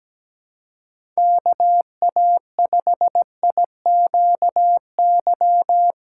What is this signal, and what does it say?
Morse code, heard as a keyed tone, about 17 wpm, 700 Hz. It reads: KA5IQY
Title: USA Call Signs - 6 Characters - 6x Character Spacing - 17wpm